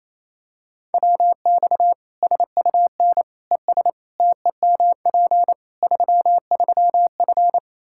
Morse code, heard as a keyed tone, about 28 wpm, 700 Hz. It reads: WX SUN ES TEMP 33F
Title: K1USN SST QSO Practice 28wpm